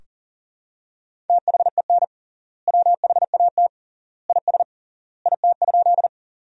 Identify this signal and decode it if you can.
Morse code, heard as a keyed tone, about 37 wpm, 700 Hz. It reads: THEN WHAT IS IT?
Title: 3x Word-Spacing - Sentences from Top 100 Words - 40wpm